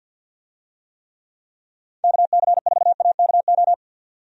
Morse code, heard as a keyed tone, about 49 wpm, 700 Hz. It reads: KC4AKO
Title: USA Call Signs - 6 Characters - 50wpm